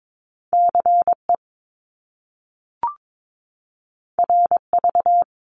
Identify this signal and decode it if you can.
Morse code, heard as a keyed tone, about 22 wpm, 700 Hz. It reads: CE RV
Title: Top Two Letter Combinations in Words 22wpm